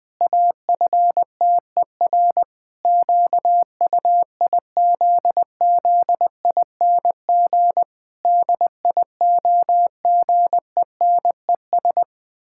Morse code, heard as a keyed tone, about 21 wpm, 700 Hz. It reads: AFTER QUIZZING DIOGENES
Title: Pangram Challenge v2 - 20wpm